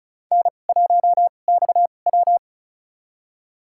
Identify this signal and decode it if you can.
Morse code, heard as a keyed tone, about 36 wpm, spaced as for 25 wpm, 700 Hz. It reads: N1XW
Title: USA Call Signs - 4 Characters - 2x Character Spacing - 35wpm